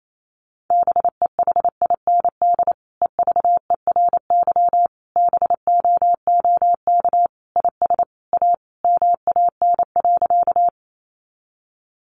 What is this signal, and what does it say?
Morse code, heard as a keyed tone, about 28 wpm, 700 Hz. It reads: BEHIND EVERY BOOK IS A MAN.
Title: Sentences from Top 500 Words 28wpm